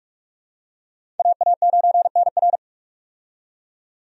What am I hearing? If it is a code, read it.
Morse code, heard as a keyed tone, about 46 wpm, 700 Hz. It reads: AA9NR
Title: USA Call Signs - 5 Characters - 45wpm